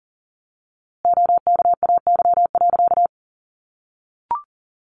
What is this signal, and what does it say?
Morse code, heard as a keyed tone, about 40 wpm, 700 Hz. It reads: OKAY.
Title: A Sound Of Thunder 40wpm